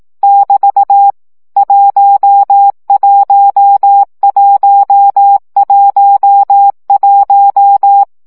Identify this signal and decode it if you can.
Morse code, heard as a keyed tone, about 18 wpm, 800 Hz. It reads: <BT> 11111
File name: CW gyakorlas 1 15 baud